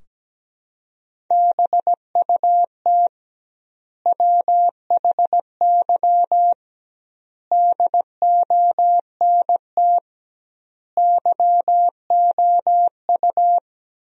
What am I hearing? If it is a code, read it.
Morse code, heard as a keyed tone, about 17 wpm, 700 Hz. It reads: BUT WHY DONT YOU
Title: 2x Word-Spacing - Sentences from Top 300 Words - 17wpm